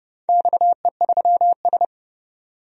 Morse code, heard as a keyed tone, about 29 wpm, 700 Hz. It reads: XE3S
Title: DXCC-100 Call Signs - 4 Characters - 30wpm